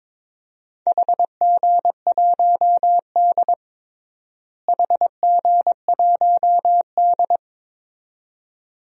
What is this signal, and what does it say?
Morse code, heard as a keyed tone, about 22 wpm, 700 Hz. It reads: HG1D HG1D
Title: DXCC-100 Call Signs - 4 Characters - Repeated 2x - 22wpm